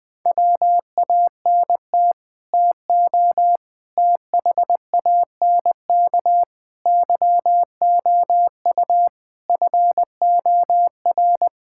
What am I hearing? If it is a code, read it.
Morse code, heard as a keyed tone, about 20 wpm, 700 Hz. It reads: WANT TO THANK YOU FOR
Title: Sets of 8 Words - 20wpm